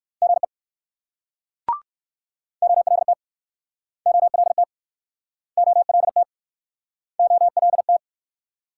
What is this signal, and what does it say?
Morse code, heard as a keyed tone, about 60 wpm, 690 Hz. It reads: GE OLT OLT OLT OLT